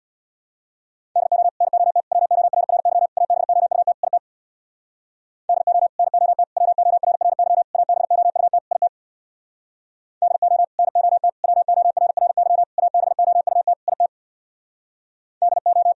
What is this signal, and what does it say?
Morse code, heard as a keyed tone, about 64 wpm, 700 Hz. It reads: DO NOT WORRY ABOUT IT DO NOT WORRY ABOUT IT DO NOT WORRY ABOUT IT DO